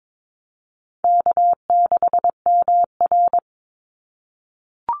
Morse code, heard as a keyed tone, about 22 wpm, 700 Hz. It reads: K6MR
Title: Call Signs - 22wpm